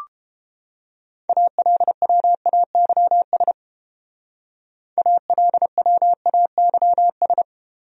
Morse code, heard as a keyed tone, about 28 wpm, 700 Hz. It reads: ALWAYS ALWAYS